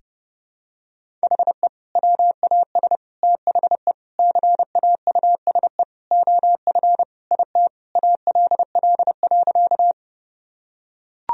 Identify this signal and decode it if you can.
Morse code, heard as a keyed tone, about 30 wpm, 700 Hz. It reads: HE WAS THE CAUSE OF IT ALL.
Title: Sentences from Top 1000 Words - Review Part 1 - 30wpm